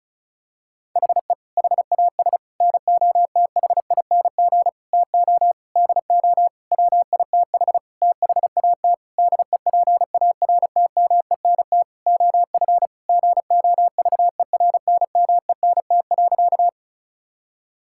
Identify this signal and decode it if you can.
Morse code, heard as a keyed tone, about 34 wpm, 700 Hz. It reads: HE HAS NOTHING TO DO WITH THAT DEPARTMENT OF GOVERNMENT.
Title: Sentences from Top 1000 Words - Review Part 5 - 35wpm